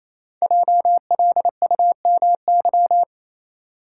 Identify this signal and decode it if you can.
Morse code, heard as a keyed tone, about 29 wpm, 700 Hz. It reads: JLUMY
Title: Code Groups - 5 Letters - 28wpm